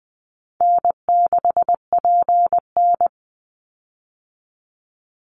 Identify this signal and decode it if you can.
Morse code, heard as a keyed tone, about 20 wpm, 700 Hz. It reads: N6PN
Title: Easy Call Signs - 20wpm